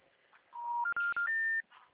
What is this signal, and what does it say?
Special information tone.